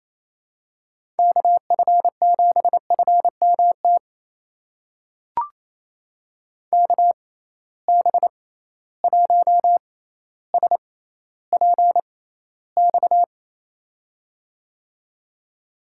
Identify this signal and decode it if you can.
Morse code, heard as a keyed tone, about 28 wpm, spaced as for 14 wpm, 700 Hz. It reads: KF7FMT KB1SPX
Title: USA Call Signs - 6 Characters - 6x Character Spacing - 28wpm